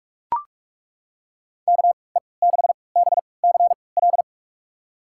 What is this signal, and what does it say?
Morse code, heard as a keyed tone, about 41 wpm, spaced as for 25 wpm, 700 Hz. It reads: KE6BCL
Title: USA Call Signs - 6 Characters - 3x Character Spacing - 45wpm